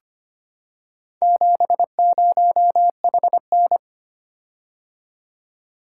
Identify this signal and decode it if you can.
Morse code, heard as a keyed tone, about 25 wpm, 700 Hz. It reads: 70HN
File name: practice / 25wpm / Four Letter-Number Combinations - 25wpm